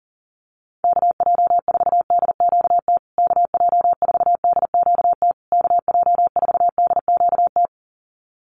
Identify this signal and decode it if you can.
Morse code, heard as a keyed tone, about 40 wpm, 700 Hz. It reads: KJ4DQT KJ4DQT KJ4DQT
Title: USA Call Signs Repeated 3x 40wpm